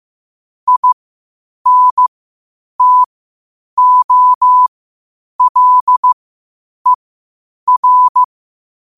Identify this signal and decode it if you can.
Morse code, heard as a keyed tone, about 15 wpm, spaced as for 8 wpm, 1000 Hz. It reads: INTOLER
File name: qotdfiles / 8wpm